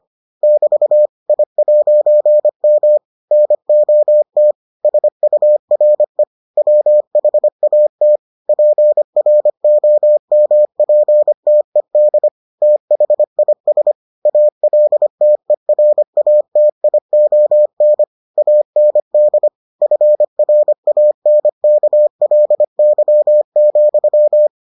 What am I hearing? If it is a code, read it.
Morse code, heard as a keyed tone, about 25 wpm, 600 Hz. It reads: <BT> I'M NOT SURE WHAT PROMPTED THIS ALTERATION AND FRANKLY,